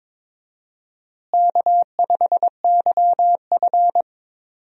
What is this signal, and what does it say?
Morse code, heard as a keyed tone, about 22 wpm, 700 Hz. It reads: K5YF
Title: USA Call Signs - 4 Characters - Repeated 2x - 22wpm